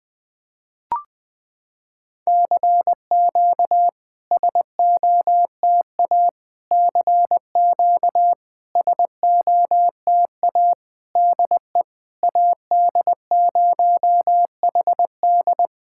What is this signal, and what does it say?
Morse code, heard as a keyed tone, about 21 wpm, 690 Hz. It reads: CQ SOTA CQ SOTA DE AD0HD
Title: SOTA QSO Practice 20wpm